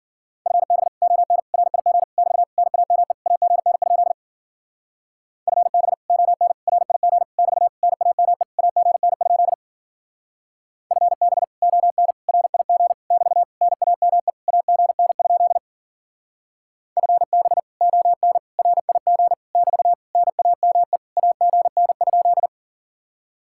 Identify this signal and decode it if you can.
Morse code, heard as a keyed tone, about 48 wpm, 700 Hz. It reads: FB ON RIG <BT> NAME AGN? FB ON RIG <BT> NAME AGN? FB ON RIG <BT> NAME AGN? FB ON RIG <BT> NAME AGN?
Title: Speed-Racing Common QSO Phrases - 40wpm